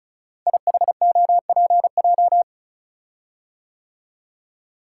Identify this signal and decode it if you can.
Morse code, heard as a keyed tone, about 36 wpm, 690 Hz. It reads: IHOPJ
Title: Code Groups - 5 Letters - 35wpm